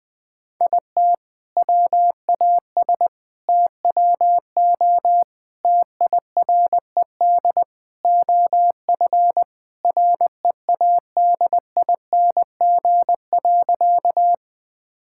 Morse code, heard as a keyed tone, about 20 wpm, 700 Hz. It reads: IT WAS TWO TIRED OF READING.
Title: All-Age Appropriate Jokes - 20wpm